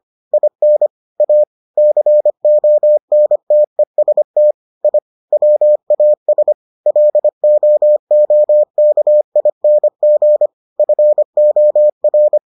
Morse code, heard as a keyed tone, about 25 wpm, 600 Hz. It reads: IN A CONTEST I WAS LOOKING FOR